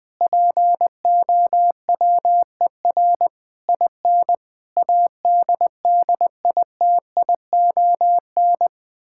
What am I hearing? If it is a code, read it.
Morse code, heard as a keyed tone, about 20 wpm, 700 Hz. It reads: POWER IN ADDITION
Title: Sentences from Top 1000 Words - Review Part 6 - 20wpm